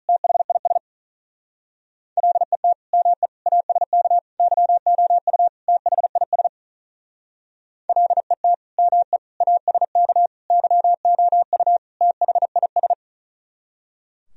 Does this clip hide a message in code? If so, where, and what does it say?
Yes: Morse code from 0.1 to 12.9 s, heard as a keyed tone, about 44 wpm, 700 Hz. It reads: THIS LET ME ASK YOU THIS LET ME ASK YOU THIS